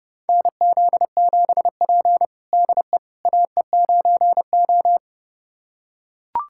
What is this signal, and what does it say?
Morse code, heard as a keyed tone, about 30 wpm, 700 Hz. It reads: NZ7P DE AE9O
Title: USA Callsign DE Callsign - 4 Characters - 30wpm